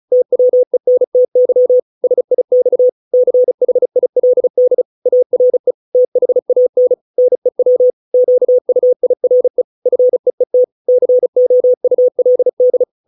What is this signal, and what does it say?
Morse code, heard as a keyed tone, about 34 wpm, 500 Hz. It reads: TWENTY SIX CHILD ARE THAN NEW QUIRE FEET COULD